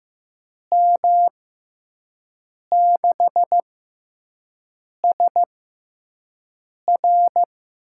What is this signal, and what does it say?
Morse code, heard as a keyed tone, about 15 wpm, spaced as for 5 wpm, 700 Hz. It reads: M6SR
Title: USA Call Signs - 5 Characters - 6x Character Spacing - 15wpm